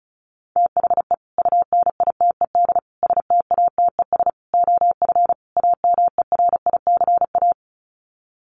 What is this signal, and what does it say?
Morse code, heard as a keyed tone, about 34 wpm, 700 Hz. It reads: THE UNITED STATES OF AMERICA
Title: Sets of 5 Words 35wpm